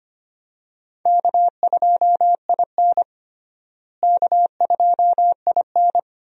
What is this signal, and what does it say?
Morse code, heard as a keyed tone, about 25 wpm, 700 Hz. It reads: K2IN K2IN
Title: USA Call Signs - 4 Characters - Repeated 2x - 25wpm